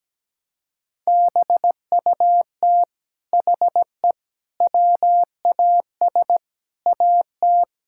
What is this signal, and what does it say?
Morse code, heard as a keyed tone, about 17 wpm, 700 Hz. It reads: BUT HE WAS AT
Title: Sentences from Top 1000 Words - Review Part 5 - 17wpm